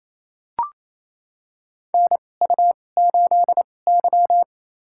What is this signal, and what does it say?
Morse code, heard as a keyed tone, about 29 wpm, spaced as for 21 wpm, 700 Hz. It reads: NU8Y
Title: USA Call Signs - 4 Characters - 2x Character Spacing - 28wpm